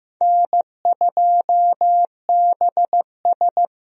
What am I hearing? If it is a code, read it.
Morse code, heard as a keyed tone, about 15 wpm, 700 Hz. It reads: N2BS
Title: USA Call Signs - 4 Characters - Repeated 3x - 15wpm